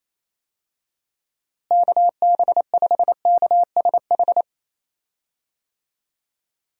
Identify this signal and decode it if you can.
Morse code, heard as a keyed tone, about 28 wpm, 700 Hz. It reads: KB5KSH
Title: USA Call Signs - 6 Characters - Repeated 2x - 28wpm